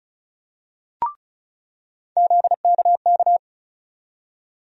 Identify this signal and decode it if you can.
Morse code, heard as a keyed tone, about 36 wpm, 700 Hz. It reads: ZKK
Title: USA Call Sign Postfixes - 3 Letters 35wpm